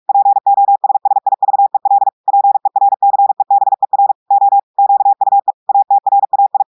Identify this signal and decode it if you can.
Morse code, heard as a keyed tone, about 46 wpm, 800 Hz. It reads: POSSIVEL PERCEBER O QUE ATRAI
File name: audios / portugais / portugais texte en clair 2 vitesse 45 mots minutes